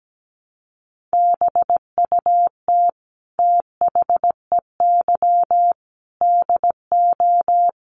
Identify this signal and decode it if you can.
Morse code, heard as a keyed tone, about 17 wpm, 700 Hz. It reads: BUT THEY DO
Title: Sentences from Top 1300 Words 17wpm